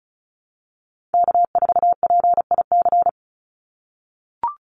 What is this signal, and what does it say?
Morse code, heard as a keyed tone, about 34 wpm, 700 Hz. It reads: K4PIC
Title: NA SOTA Chaser Callsigns 35wpm